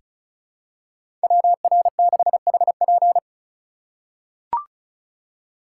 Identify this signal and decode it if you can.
Morse code, heard as a keyed tone, about 34 wpm, 700 Hz. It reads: WR6HP
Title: USA Call Signs - 5 Characters - 4x Character Spacing - 35wpm